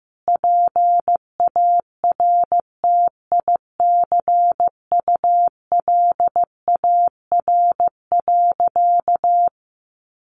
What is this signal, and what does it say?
Morse code, heard as a keyed tone, about 15 wpm, 700 Hz. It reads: PARTICULAR.
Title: Sentences from Top 600 Words 15wpm